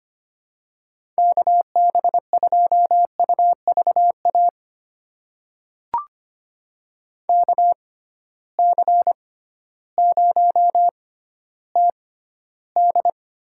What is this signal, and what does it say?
Morse code, heard as a keyed tone, about 25 wpm, spaced as for 13 wpm, 700 Hz. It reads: KB2UVA KC0TD
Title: USA Call Signs - 6 Characters - 6x Character Spacing - 25wpm